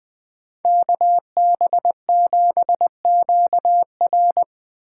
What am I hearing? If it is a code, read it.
Morse code, heard as a keyed tone, about 20 wpm, 700 Hz. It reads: KB7QR
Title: Speed Racing - USA 2x2 Call Signs 17wpm